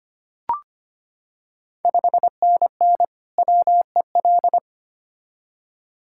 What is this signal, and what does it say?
Morse code, heard as a keyed tone, about 25 wpm, 700 Hz. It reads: 5NN WEL
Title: Ontario QSO Party Exchange 25wpm